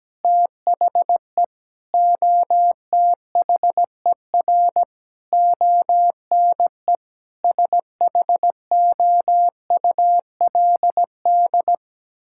Morse code, heard as a keyed tone, about 17 wpm, 700 Hz. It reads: THE OTHER ONE SHOULD
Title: Sentences from Top 1600 Words 17wpm